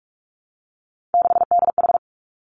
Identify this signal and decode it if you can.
Morse code, heard as a keyed tone, about 46 wpm, 700 Hz. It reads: 6DH
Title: Three Challenging Letter-Number Combinations 45wpm